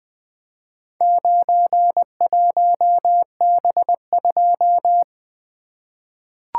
Morse code, heard as a keyed tone, about 20 wpm, 700 Hz. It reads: 91B2
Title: Four Letter-Number Combinations - 20wpm